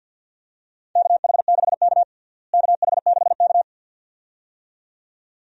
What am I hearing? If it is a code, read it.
Morse code, heard as a keyed tone, about 50 wpm, 700 Hz. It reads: KH6K KH6K
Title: DXCC-100 Call Signs - 4 Characters - Repeated 2x - 50wpm